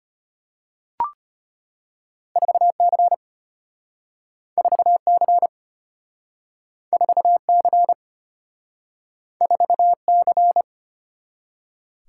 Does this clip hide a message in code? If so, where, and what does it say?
Yes: Morse code from 2.4 to 10.6 s, heard as a keyed tone, about 35 wpm, 700 Hz. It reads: 4 C 4 C 4 C 4 C